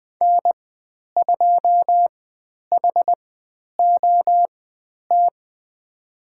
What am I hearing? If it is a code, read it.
Morse code, heard as a keyed tone, about 20 wpm, spaced as for 10 wpm, 700 Hz. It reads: N2HOT
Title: 102 Lesson 17g - Letter u - Call Signs 20wpm